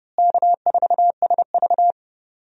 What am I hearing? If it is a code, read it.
Morse code, heard as a keyed tone, about 30 wpm, 700 Hz. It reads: K4SV
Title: USA Call Signs - 4 Characters - Repeated 2x - 30wpm